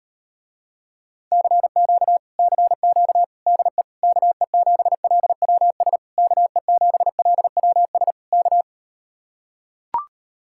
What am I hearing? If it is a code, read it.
Morse code, heard as a keyed tone, about 38 wpm, 700 Hz. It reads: CQ CQ DE KE7LWS KE7LWS K